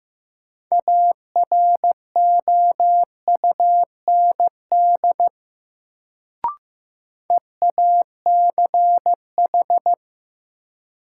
Morse code, heard as a keyed tone, about 15 wpm, 700 Hz. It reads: AROUND EACH
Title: Sets of 2 Words - New Word Drill - 15wpm